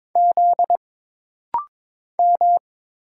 Morse code, heard as a keyed tone, about 22 wpm, 700 Hz. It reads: Z M
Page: This rapid-fire timing is geared to push your ICR (Instant Character Recognition) skills.